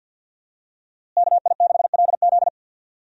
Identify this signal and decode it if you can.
Morse code, heard as a keyed tone, about 46 wpm, 700 Hz. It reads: KI6LZ
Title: USA Call Signs - 5 Characters - Repeated 3x - 50wpm